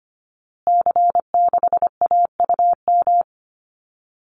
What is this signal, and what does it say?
Morse code, heard as a keyed tone, about 25 wpm, 700 Hz. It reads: C6AUM
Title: Call Signs Repeated 3x - 25wpm